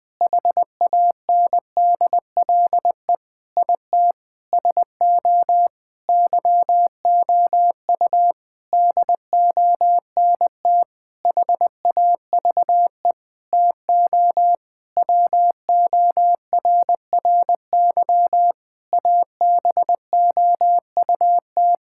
Morse code, heard as a keyed tone, about 20 wpm, 700 Hz. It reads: HANDLE IT SO YOU DONT HAVE TO WORRY ABOUT
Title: Sentences with 7-Word Phrases - Spoken Beginning - 20wpm